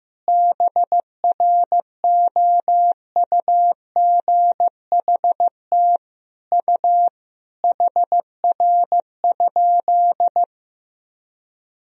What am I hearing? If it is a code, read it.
Morse code, heard as a keyed tone, about 14 wpm, 700 Hz. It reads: BROUGHT U HR?
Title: Sentences from Top 500 Words and Abbreviations 15wpm